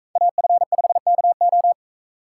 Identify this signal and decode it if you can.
Morse code, heard as a keyed tone, about 41 wpm, 700 Hz. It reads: AF5KO